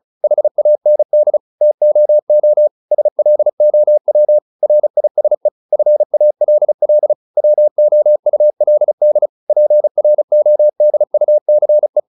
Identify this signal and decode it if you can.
Morse code, heard as a keyed tone, about 34 wpm, 600 Hz. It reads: HAND TOO SLOW RISE FALL WOULD PRODUCE